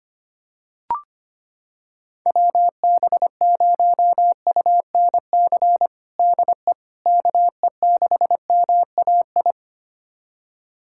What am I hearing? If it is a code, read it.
Morse code, heard as a keyed tone, about 25 wpm, 700 Hz. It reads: WB0UNC DE KE6MAI
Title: USA Callsign DE Callsign 25wpm